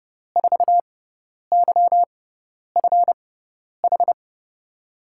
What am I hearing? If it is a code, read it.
Morse code, heard as a keyed tone, about 31 wpm, spaced as for 10 wpm, 700 Hz. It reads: 4YFH
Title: USA Call Signs - 6 Characters - 6x Character Spacing - 30wpm